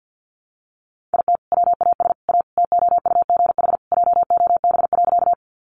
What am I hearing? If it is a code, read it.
Morse code, heard as a keyed tone, about 50 wpm, 700 Hz. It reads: IT WAS A TOUGH JOB.
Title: Sentences from Top 1200 Words 50wpm